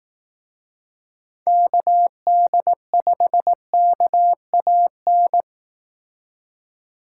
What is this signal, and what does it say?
Morse code, heard as a keyed tone, about 18 wpm, 700 Hz. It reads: KD5KAN
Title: Speed Racing - USA Call Signs 15wpm